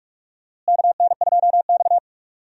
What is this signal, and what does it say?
Morse code, heard as a keyed tone, about 46 wpm, 700 Hz. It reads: KN2X
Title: USA Call Signs - 4 Characters - Repeated 3x - 45wpm